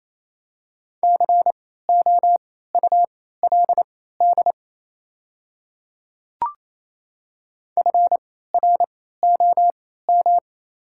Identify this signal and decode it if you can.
Morse code, heard as a keyed tone, about 28 wpm, spaced as for 16 wpm, 700 Hz. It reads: COULD FROM
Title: 3x Character Spacing - Top 100 Words - 28wpm